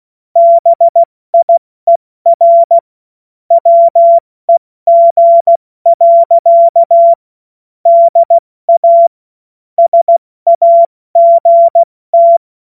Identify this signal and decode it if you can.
Morse code, heard as a keyed tone, about 16 wpm, spaced as for 14 wpm, 680 Hz. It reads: BIER WEG. DA SAGT
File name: Morsekurs / Witze / Witz 22. Februar 2022 70 F80